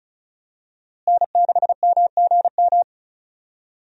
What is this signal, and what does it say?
Morse code, heard as a keyed tone, about 36 wpm, 700 Hz. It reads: N6MGM
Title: USA Call Signs - 5 Characters - Repeated 3x - 35wpm